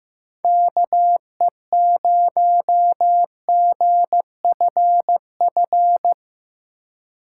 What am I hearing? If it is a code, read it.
Morse code, heard as a keyed tone, about 16 wpm, 700 Hz. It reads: KE0GFF
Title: USA Call Signs - 4x Character Spacing - 15wpm